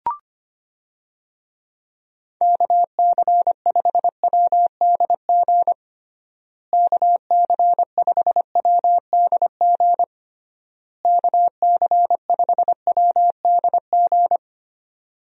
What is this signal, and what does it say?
Morse code, heard as a keyed tone, about 25 wpm, 700 Hz. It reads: KC5WDG KC5WDG KC5WDG
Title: USA Call Signs - 6 Characters - Repeated 3x - 25wpm